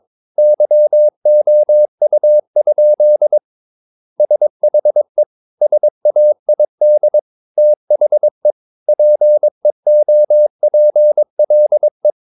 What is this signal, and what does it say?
Morse code, heard as a keyed tone, about 22 wpm, 600 Hz. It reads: YOU? SHE SAID THE PEOPLE
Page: Romanen Never af Ken Follet på engelsk.